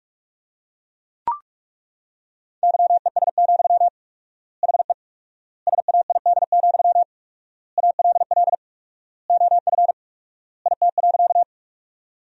Morse code, heard as a keyed tone, about 41 wpm, 700 Hz. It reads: YES, HE SAID, ALL OF IT.
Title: 4x Word-Spacing - Sentences from Top 500 Words - 45wpm